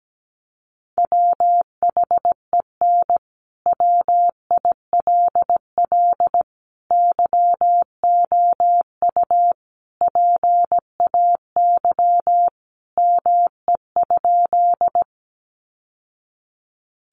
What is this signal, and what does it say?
Morse code, heard as a keyed tone, about 17 wpm, 700 Hz. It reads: WHEN WILL YOU PAY ME?
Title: Sentences from Top 700 Words 17wpm